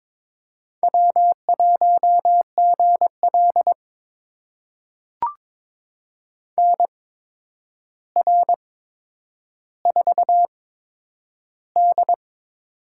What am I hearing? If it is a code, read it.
Morse code, heard as a keyed tone, about 22 wpm, spaced as for 9 wpm, 700 Hz. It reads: W1GL NR4D
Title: USA Call Signs - 4 Characters - 8x Character Spacing - 22wpm